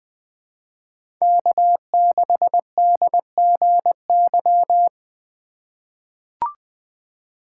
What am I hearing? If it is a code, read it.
Morse code, heard as a keyed tone, about 20 wpm, 700 Hz. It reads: K6DGY
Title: USA Call Signs - 8x Character Spacing - 20wpm